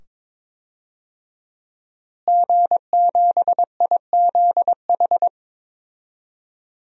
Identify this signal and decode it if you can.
Morse code, heard as a keyed tone, about 22 wpm, 700 Hz. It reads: G7IZH
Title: DXCC-100 Call Signs - 5 Characters - Repeated 2x - 22wpm